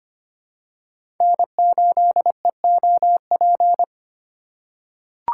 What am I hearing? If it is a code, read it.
Morse code, heard as a keyed tone, about 25 wpm, 700 Hz. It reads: N8EOP
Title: USA Call Signs - 5 Characters - 4x Character Spacing - 25wpm